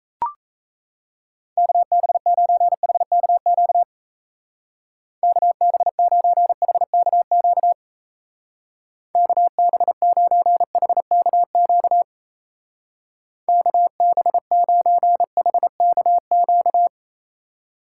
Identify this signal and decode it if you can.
Morse code, heard as a keyed tone, about 34 wpm, 700 Hz. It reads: KB9HKQ KB9HKQ KB9HKQ KB9HKQ
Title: Speed Racing - USA 2x3 Call Signs 28wpm